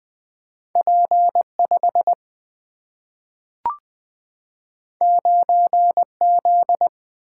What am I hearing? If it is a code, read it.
Morse code, heard as a keyed tone, about 20 wpm, 700 Hz. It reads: P5 9Z
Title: Single-Letter Single-Number 20wpm